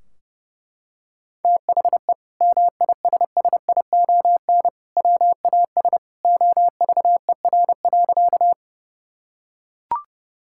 Morse code, heard as a keyed tone, about 28 wpm, 700 Hz. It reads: THE MISSION WAS OVER.
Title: Sentences from Top 1600 Words 30wpm